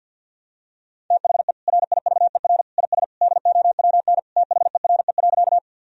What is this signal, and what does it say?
Morse code, heard as a keyed tone, about 53 wpm, 700 Hz. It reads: THE RIVER IS DOWN THERE.
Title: Sentences from Top 1000 Words - Review Part 1 - 50wpm